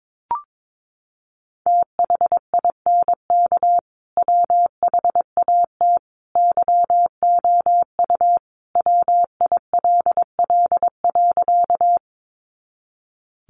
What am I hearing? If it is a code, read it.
Morse code, heard as a keyed tone, about 22 wpm, 700 Hz. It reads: THINK WHAT YOU WILL.
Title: Sentences from Top 200 Words 22wpm